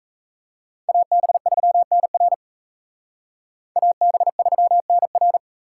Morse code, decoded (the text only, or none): AB3NR AB3NR